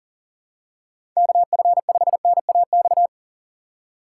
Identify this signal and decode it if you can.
Morse code, heard as a keyed tone, about 40 wpm, 700 Hz. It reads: KF5NAX
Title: USA Call Signs - 4x Character Spacing - 40wpm